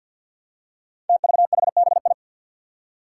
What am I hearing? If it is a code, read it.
Morse code, heard as a keyed tone, about 45 wpm, 700 Hz. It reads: TVHBI
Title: Five Challenging Letter-Number Combinations 50wpm